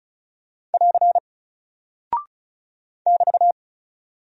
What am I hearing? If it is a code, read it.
Morse code, heard as a keyed tone, about 36 wpm, 700 Hz. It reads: <AR> <BT>
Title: Single Character (Letter:Number:Punctuation:Prosign) Alt 35wpm